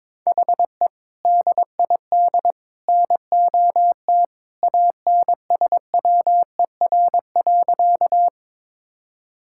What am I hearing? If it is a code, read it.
Morse code, heard as a keyed tone, about 22 wpm, 700 Hz. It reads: HE DID NOT ANSWER.
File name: practice / 22wpm / Sentences with 3-Word Phrases - 22wpm